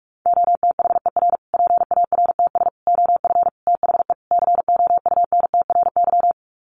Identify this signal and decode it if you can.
Morse code, heard as a keyed tone, about 44 wpm, 700 Hz. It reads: OTHER PARTS OF THE COUNTRY
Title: Sets of 5 Words Encore 45wpm